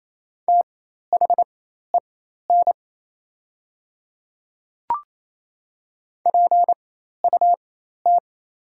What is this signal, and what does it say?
Morse code, heard as a keyed tone, about 27 wpm, spaced as for 13 wpm, 700 Hz. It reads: THEN PUT
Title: 4x Character Spacing - Top 300 Words - 28wpm